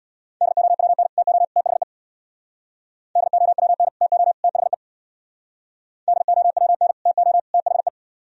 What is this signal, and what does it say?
Morse code, heard as a keyed tone, about 67 wpm, 700 Hz. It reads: DOWN TO THE DOWN TO THE DOWN TO THE